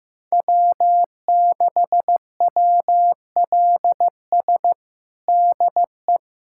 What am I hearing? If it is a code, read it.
Morse code, heard as a keyed tone, about 15 wpm, 700 Hz. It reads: W6WLS DE
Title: USA Callsign DE Callsign - 5 Characters - 15wpm